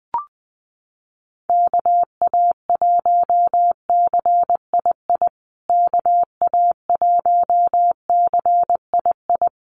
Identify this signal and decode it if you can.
Morse code, heard as a keyed tone, about 20 wpm, 700 Hz. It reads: KA1CII KA1CII
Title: USA Call Signs Repeated 3x 20wpm